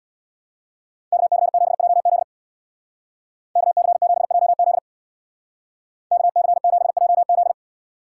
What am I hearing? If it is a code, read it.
Morse code, heard as a keyed tone, about 60 wpm, 700 Hz. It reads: KC7JZ KC7JZ KC7JZ
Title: Speed Racing - USA 2x2 Call Signs 50wpm